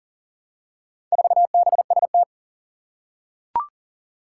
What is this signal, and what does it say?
Morse code, heard as a keyed tone, about 40 wpm, 700 Hz. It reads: 4BST
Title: Four Challenging Letter-Number Combinations 40wpm